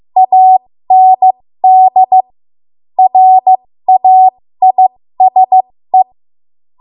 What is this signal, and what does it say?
Morse code, heard as a keyed tone, about 15 wpm, spaced as for 13 wpm, 750 Hz. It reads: AND RAISE
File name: Morse / 13WPM